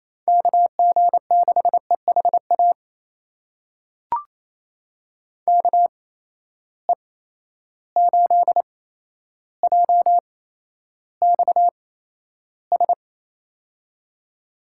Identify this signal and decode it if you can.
Morse code, heard as a keyed tone, about 28 wpm, spaced as for 12 wpm, 700 Hz. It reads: KG6EHA KE8JXS
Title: USA Call Signs - 6 Characters - 8x Character Spacing - 28wpm